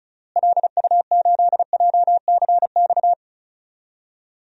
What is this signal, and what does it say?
Morse code, heard as a keyed tone, about 36 wpm, 700 Hz. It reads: LU8JCX
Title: DXCC-100 Call Signs - 6 Characters - Repeated 3x - 35wpm